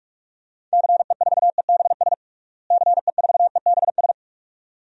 Morse code, heard as a keyed tone, about 48 wpm, 700 Hz. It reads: CE4EBS CE4EBS
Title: DXCC-100 Call Signs - 6 Characters - Repeated 2x - 45wpm